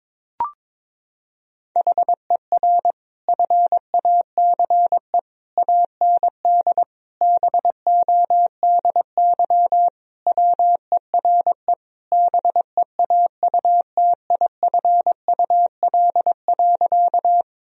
Morse code, heard as a keyed tone, about 22 wpm, 700 Hz. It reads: HER FACE AND BODY WERE BEAUTIFUL.
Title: Sentences from Top 1000 Words - Review Part 2 - 22wpm